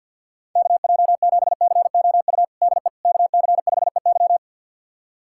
Morse code, heard as a keyed tone, about 46 wpm, 690 Hz. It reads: KJ7COU DE KC5EY
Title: USA Callsign DE Callsign 50wpm